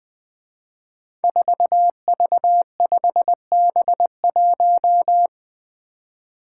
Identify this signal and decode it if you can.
Morse code, heard as a keyed tone, about 20 wpm, 700 Hz. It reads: 4V5B1
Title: Five Challenging Letter-Number Combinations 20wpm